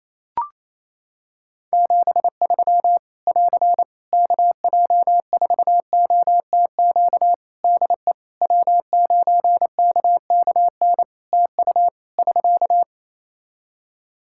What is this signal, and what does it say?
Morse code, heard as a keyed tone, about 28 wpm, 700 Hz. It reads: 73 <AR> KJ4OTQ DE W9KKN TU <SK>
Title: Practice QSO - Exchange 4 - Ending - 28wpm